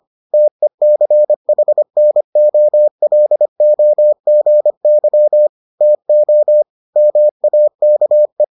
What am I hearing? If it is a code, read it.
Morse code, heard as a keyed tone, about 25 wpm, 600 Hz. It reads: TECHNOLOGY TO MAKE